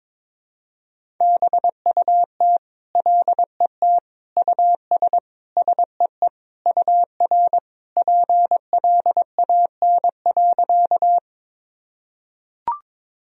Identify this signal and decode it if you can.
Morse code, heard as a keyed tone, about 22 wpm, 700 Hz. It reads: BUT LET US SEE UR PLAN.
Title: Sentences from Top 500 Words and Abbreviations 22wpm